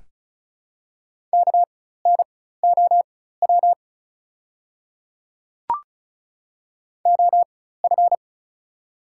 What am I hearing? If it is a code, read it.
Morse code, heard as a keyed tone, about 36 wpm, spaced as for 16 wpm, 700 Hz. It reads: KNOW OF
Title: 4x Character Spacing - Top 100 Words - 35wpm